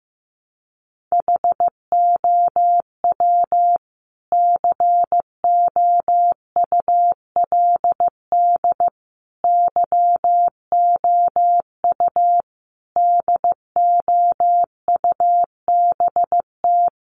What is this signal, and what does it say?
Morse code, heard as a keyed tone, about 15 wpm, 700 Hz. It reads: HOW COULD YOU DOUBT
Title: Sentences from Top 1500 Words 15wpm